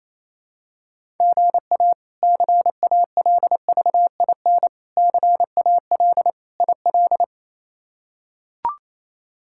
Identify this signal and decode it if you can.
Morse code, heard as a keyed tone, about 28 wpm, 700 Hz. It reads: GA CALVIN CAL IL
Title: K1USN SST QSO Practice - 28wpm